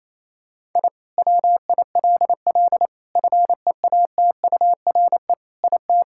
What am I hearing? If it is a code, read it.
Morse code, heard as a keyed tone, about 28 wpm, 700 Hz. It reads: I WILL FEATURE IT
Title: Sentences from Top 1000 Words - Review Part 5 - 28wpm